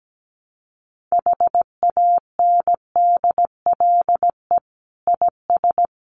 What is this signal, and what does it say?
Morse code, heard as a keyed tone, about 17 wpm, 700 Hz. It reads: HANDLE IS
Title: Common QSO Phrases 17wpm